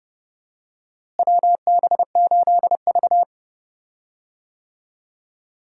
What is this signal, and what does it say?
Morse code, heard as a keyed tone, about 30 wpm, 700 Hz. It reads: WB8V
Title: USA Call Signs - 4 Characters - Repeated 3x - 30wpm